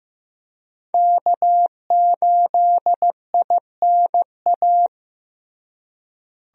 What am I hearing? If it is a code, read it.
Morse code, heard as a keyed tone, about 15 wpm, 700 Hz. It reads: K8INA
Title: USA Call Signs - 5 Characters - Repeated 2x - 15wpm